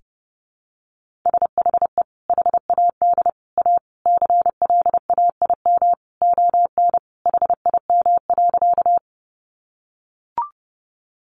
Morse code, heard as a keyed tone, about 30 wpm, 700 Hz. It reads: SHE HAD A CLAIM ON HIM.
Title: Sentences from Top 1500 Words 30wpm